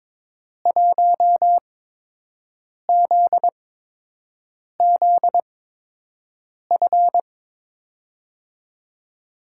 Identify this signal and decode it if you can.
Morse code, heard as a keyed tone, about 22 wpm, spaced as for 6 wpm, 700 Hz. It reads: 1ZZF
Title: USA Call Signs - 6 Characters - 8x Character Spacing - 22wpm